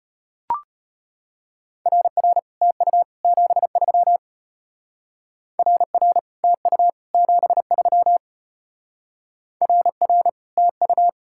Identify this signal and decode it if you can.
Morse code, heard as a keyed tone, about 32 wpm, 700 Hz. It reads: RR TU 73 RR TU 73 RR TU